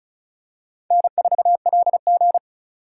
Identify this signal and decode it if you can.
Morse code, heard as a keyed tone, about 34 wpm, 700 Hz. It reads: N4LG
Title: USA Call Signs - 4 Characters - 2x Character Spacing - 35wpm